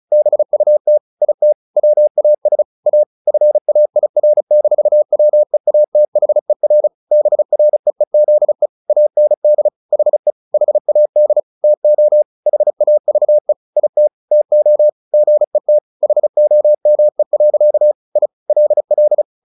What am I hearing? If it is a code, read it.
Morse code, heard as a keyed tone, about 34 wpm, 600 Hz. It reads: BUT IT WAS A FAIR-WEATHER BREEZE AND HE HAD TO HAVE IT TO GET HOME. I LL
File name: the old man and the sea 2 35wpm